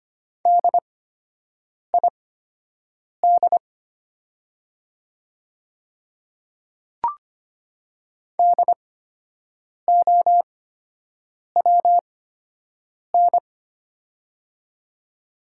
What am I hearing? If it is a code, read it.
Morse code, heard as a keyed tone, about 25 wpm, spaced as for 7 wpm, 700 Hz. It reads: DID DOWN
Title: 8x Character Spacing - Top 200 Words - 25wpm